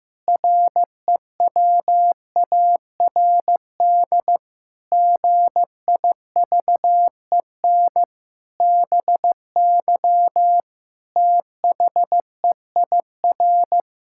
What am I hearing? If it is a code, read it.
Morse code, heard as a keyed tone, about 15 wpm, 700 Hz. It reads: REWARD GIVEN BY THEIR
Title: Pangram Challenge v2 - 15wpm